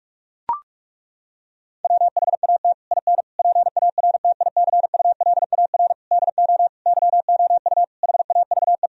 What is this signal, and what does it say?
Morse code, heard as a keyed tone, about 41 wpm, 700 Hz. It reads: WHAT IN PARTICULAR DO YOU HAVE
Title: Sentences from Top 1000 Words - Review Part 4 - 45wpm